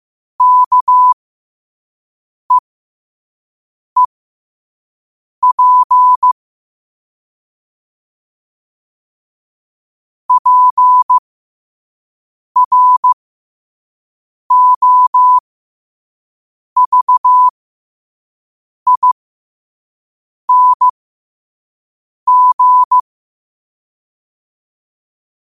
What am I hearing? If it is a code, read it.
Morse code, heard as a keyed tone, about 15 wpm, spaced as for 5 wpm, 1000 Hz. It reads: KEEP PROVING
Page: New quotes every day in morse code at 5 Words per minute.